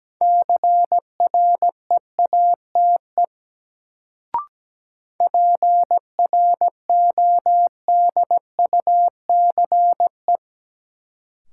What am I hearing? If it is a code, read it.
Morse code, heard as a keyed tone, about 17 wpm, 700 Hz. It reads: CREATE PRODUCE
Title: Sentences from Top 1000 Words - Review Part 3 - New Word Drill - 17wpm